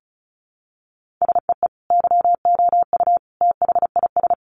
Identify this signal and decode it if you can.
Morse code, heard as a keyed tone, about 34 wpm, 700 Hz. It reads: SEE YOU THIS
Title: Sentences from Top 1500 Words 35wpm